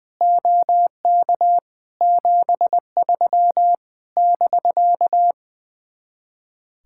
Morse code, heard as a keyed tone, about 21 wpm, 700 Hz. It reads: OK 73 <BK>
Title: Speed-Racing Common QSO Phrases Encore - 20wpm